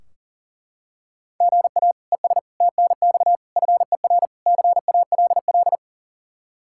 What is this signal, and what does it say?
Morse code, heard as a keyed tone, about 37 wpm, 700 Hz. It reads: GA ES TNX FER CALL
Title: Practice QSO - Exchange 1 - Beginning - 40wpm